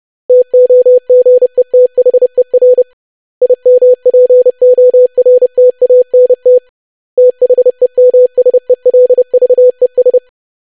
Morse code, heard as a keyed tone, about 30 wpm, 500 Hz. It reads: TOGETHER IMPORTANT THEMSELVES
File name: lesson / web / basic500-25wpm